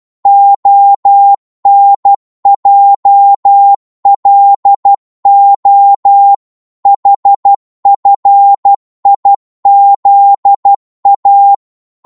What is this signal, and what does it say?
Morse code, heard as a keyed tone, about 12 wpm, 800 Hz. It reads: ONJLO HFIZA
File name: Morse / ltr-12wpm